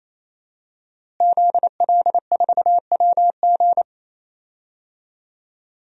Morse code, heard as a keyed tone, about 28 wpm, 700 Hz. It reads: ZL4WG
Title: DXCC-100 Call Signs - 5 Characters - Repeated 3x - 28wpm